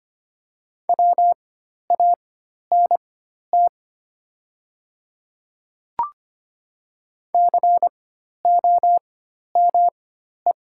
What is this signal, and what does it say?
Morse code, heard as a keyed tone, about 25 wpm, spaced as for 12 wpm, 700 Hz. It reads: WANT COME
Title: 4x Character Spacing - Top 200 Words - 25wpm